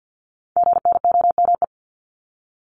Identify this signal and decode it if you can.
Morse code, heard as a keyed tone, about 50 wpm, 700 Hz. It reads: GNOME
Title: Multi-letter Phonemes 50wpm